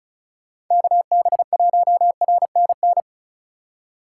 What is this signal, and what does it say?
Morse code, heard as a keyed tone, about 43 wpm, 700 Hz. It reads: KB1RNN
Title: Speed Racing - USA Call Signs 35wpm